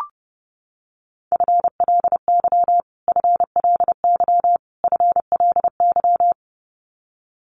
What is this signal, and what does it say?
Morse code, heard as a keyed tone, about 30 wpm, 700 Hz. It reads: FLY FLY FLY
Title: Top 500 Words Repeated 30wpm